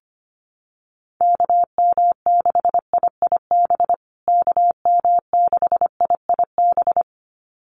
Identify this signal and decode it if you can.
Morse code, heard as a keyed tone, about 25 wpm, 700 Hz. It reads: KM6IIB KM6IIB
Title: USA Call Signs Repeated 2x 25wpm